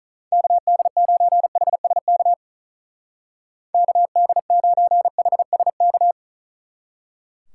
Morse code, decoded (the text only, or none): KD9HSK KD9HSK